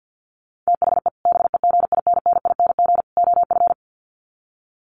Morse code, heard as a keyed tone, about 50 wpm, 700 Hz. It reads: THE BEGINNING OF
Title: Sets of 3 Words 50wpm